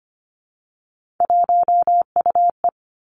Morse code, heard as a keyed tone, about 25 wpm, 700 Hz. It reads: 1UE
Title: Three Challenging Letter-Number Combinations 25wpm